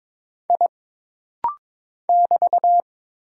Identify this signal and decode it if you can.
Morse code, heard as a keyed tone, about 22 wpm, 700 Hz. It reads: I <BT>
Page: This rapid-fire timing is geared to push your ICR (Instant Character Recognition) skills.